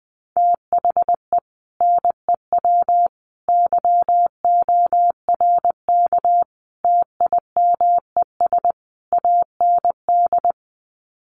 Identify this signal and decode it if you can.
Morse code, heard as a keyed tone, about 20 wpm, 700 Hz. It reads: THE NEW YORK TIMES AND
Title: Sets of 5 Words Encore 20wpm